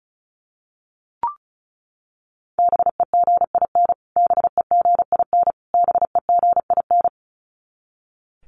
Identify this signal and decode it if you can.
Morse code, heard as a keyed tone, about 34 wpm, 700 Hz. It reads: BEGIN BEGIN BEGIN
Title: Top 500 Words Repeated 35wpm